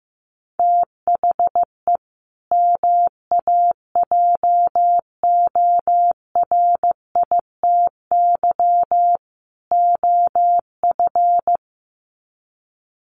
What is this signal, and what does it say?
Morse code, heard as a keyed tone, about 15 wpm, 700 Hz. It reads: THE MAJORITY OF
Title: Sets of 3 Words 15wpm